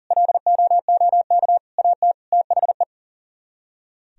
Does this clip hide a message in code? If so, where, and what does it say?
Yes: Morse code from 0.1 to 2.8 s, heard as a keyed tone, about 52 wpm, 700 Hz. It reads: LOOK AT THE